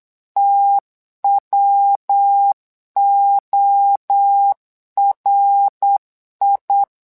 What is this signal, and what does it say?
Morse code, heard as a keyed tone, about 8 wpm, 800 Hz. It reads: TWORI
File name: LEZIONI / 3 MWRO / 3 MWRO 8wpm 2606